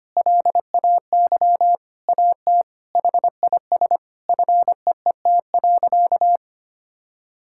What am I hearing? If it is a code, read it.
Morse code, heard as a keyed tone, about 25 wpm, 700 Hz. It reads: LAY AT HIS FEET.
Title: Sentences from Top 1000 Words - Review Part 5 - 25wpm